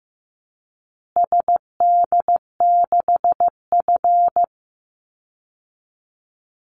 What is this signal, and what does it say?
Morse code, heard as a keyed tone, about 15 wpm, 700 Hz. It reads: SD6F
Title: Call Signs Repeated 2x - 15wpm